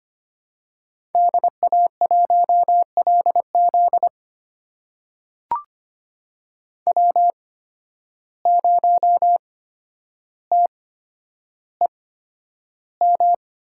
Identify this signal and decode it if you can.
Morse code, heard as a keyed tone, about 25 wpm, spaced as for 11 wpm, 700 Hz. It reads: DA1LZ W0TEM
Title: USA Call Signs - 8x Character Spacing - 25wpm